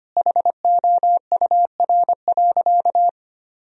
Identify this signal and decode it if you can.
Morse code, heard as a keyed tone, about 25 wpm, 700 Hz. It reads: HOUR.
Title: Sentences from Top 1000 Words - Review Part 2 - 25wpm